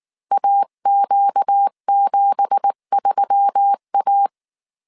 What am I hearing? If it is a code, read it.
Morse code, heard as a keyed tone, about 19 wpm, 800 Hz. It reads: AQ73A
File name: CW / DET / 1 9 1500C S B 18wpm 2551